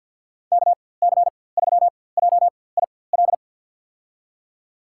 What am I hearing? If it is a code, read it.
Morse code, heard as a keyed tone, about 49 wpm, spaced as for 23 wpm, 700 Hz. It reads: KC3JIL
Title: USA Call Signs - 4x Character Spacing - 50wpm